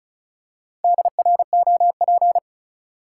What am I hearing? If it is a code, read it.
Morse code, heard as a keyed tone, about 36 wpm, 700 Hz. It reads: DROP
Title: Four Letter Words - 35wpm